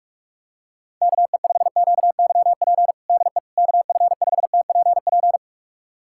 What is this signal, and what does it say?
Morse code, heard as a keyed tone, about 46 wpm, 700 Hz. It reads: KE5QYP DE KF5TPP
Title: USA Callsign DE Callsign - 6 Characters - 45wpm